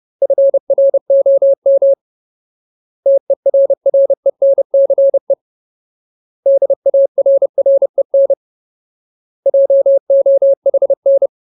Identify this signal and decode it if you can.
Morse code, heard as a keyed tone, about 30 wpm, 550 Hz. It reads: FROM TERRENCE DARREN JOHN
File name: Ultimate Word List / 30WPM Ultimate